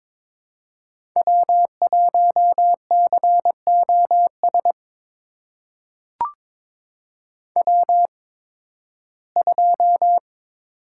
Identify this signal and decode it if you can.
Morse code, heard as a keyed tone, about 22 wpm, spaced as for 14 wpm, 700 Hz. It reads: W1COS W2
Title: USA Call Signs - 5 Characters - 8x Character Spacing - 22wpm